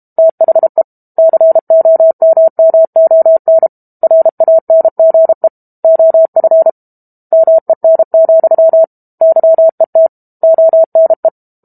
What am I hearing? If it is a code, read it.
Morse code, heard as a keyed tone, about 32 wpm, 650 Hz. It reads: THE COMMON RANGE OF MEN, YET ONE
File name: Book Two / 06-Chapter 6 - 32 WPM